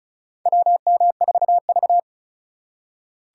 Morse code, heard as a keyed tone, about 34 wpm, 690 Hz. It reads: WM4V
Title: USA Call Signs - 4 Characters - 2x Character Spacing - 35wpm